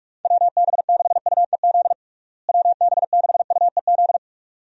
Morse code, heard as a keyed tone, about 46 wpm, 700 Hz. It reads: WB6UEZ WB6UEZ
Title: USA Call Signs - 6 Characters - Repeated 2x - 45wpm